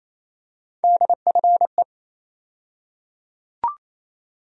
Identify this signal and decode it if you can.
Morse code, heard as a keyed tone, about 27 wpm, 700 Hz. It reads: DFE
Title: USA Call Sign Postfixes - 3 Letters 28wpm